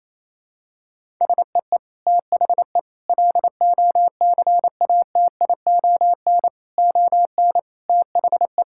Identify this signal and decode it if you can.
Morse code, heard as a keyed tone, about 28 wpm, 700 Hz. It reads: SEE THE LOCATION ON THE
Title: Sentences from Top 1000 Words - Review Part 6 - 28wpm